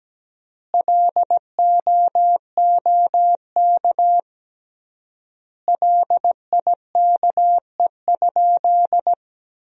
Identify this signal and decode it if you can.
Morse code, heard as a keyed tone, about 17 wpm, 700 Hz. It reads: LOOK LIKE?
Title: 3x Word-Spacing - Sentences from Top 100 Words - 17wpm